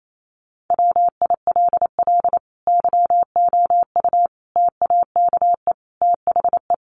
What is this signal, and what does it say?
Morse code, heard as a keyed tone, about 28 wpm, 700 Hz. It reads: WILL YOU TAKE THE
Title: Sentences from Top 1100 Words 28wpm